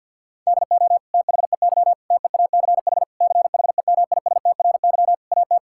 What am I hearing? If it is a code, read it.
Morse code, heard as a keyed tone, about 50 wpm, 700 Hz. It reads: DO THEY TEACH CHEMISTRY AT
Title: Sentences from Top 1000 Words - Review Part 6 - 50wpm